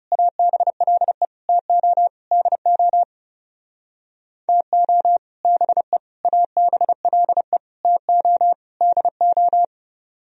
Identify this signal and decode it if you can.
Morse code, heard as a keyed tone, about 32 wpm, 700 Hz. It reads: ABLE TO DO TO BE ABLE TO DO
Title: Speed-Racing - Sets of 5 Words 30wpm